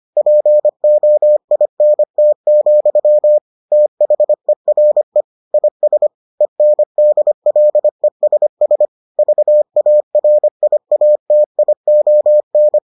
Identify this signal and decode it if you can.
Morse code, heard as a keyed tone, about 25 wpm, 600 Hz. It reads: POINT, THERE IS ENDLESS VARIATION